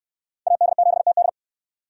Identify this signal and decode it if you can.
Morse code, heard as a keyed tone, about 66 wpm, 700 Hz. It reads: AD7TD